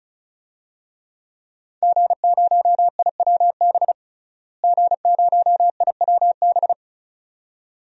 Morse code, heard as a keyed tone, about 36 wpm, 700 Hz. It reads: G0IWB G0IWB
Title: DXCC-100 Call Signs - 5 Characters - Repeated 2x - 35wpm